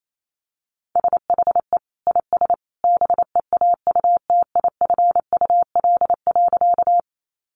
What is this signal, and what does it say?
Morse code, heard as a keyed tone, about 28 wpm, 700 Hz. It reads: SHE IS BEAUTIFUL.
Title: Sentences from Top 900 Words 28wpm